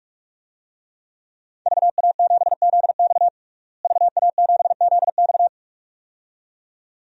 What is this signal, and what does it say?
Morse code, heard as a keyed tone, about 44 wpm, 700 Hz. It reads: VA7ZX VA7ZX
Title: DXCC-100 Call Signs - 5 Characters - Repeated 2x - 45wpm